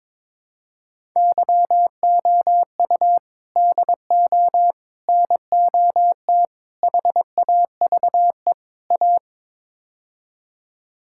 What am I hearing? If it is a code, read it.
Morse code, heard as a keyed tone, about 21 wpm, 700 Hz. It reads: YOU DO NOT HAVE A
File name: practice / 22wpm / Build A Phrase - 5-Word Phrases - 22wpm